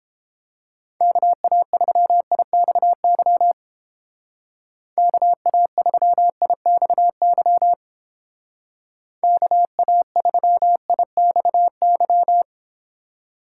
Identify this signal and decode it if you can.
Morse code, heard as a keyed tone, about 29 wpm, 700 Hz. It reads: KA3IXY KA3IXY KA3IXY